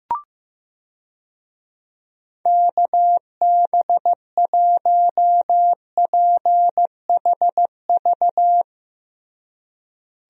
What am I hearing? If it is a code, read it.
Morse code, heard as a keyed tone, about 15 wpm, 700 Hz. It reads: KB1PHV
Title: USA Call Signs - 6 Characters - Repeated 2x - 15wpm